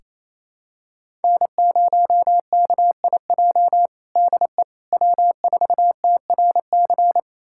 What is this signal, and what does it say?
Morse code, heard as a keyed tone, about 27 wpm, 700 Hz. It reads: N0KIJ DE W4TRC
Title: USA Callsign DE Callsign - 5 Characters - 28wpm